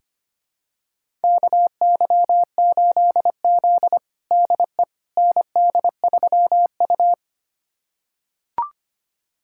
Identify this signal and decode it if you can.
Morse code, heard as a keyed tone, about 25 wpm, 700 Hz. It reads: KY8Z DE ND3U
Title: USA Callsign DE Callsign - 4 Characters - 25wpm